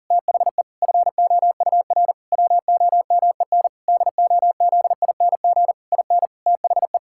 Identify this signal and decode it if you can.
Morse code, heard as a keyed tone, about 40 wpm, 700 Hz. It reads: THE FOUR WOMEN DOZING IN THE
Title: Pangram Challenge v2 - 40wpm